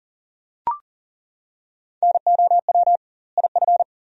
Morse code, heard as a keyed tone, about 40 wpm, 700 Hz. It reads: NOW IF
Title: 2x Word-Spacing - Sentences from Top 500 Words - 40wpm